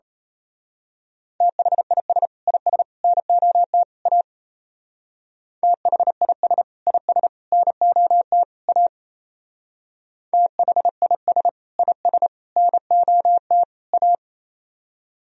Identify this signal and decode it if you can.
Morse code, heard as a keyed tone, about 32 wpm, 690 Hz. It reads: THIS IS NOT A THIS IS NOT A THIS IS NOT A
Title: Speed-Racing - Sets of 4 Words 28wpm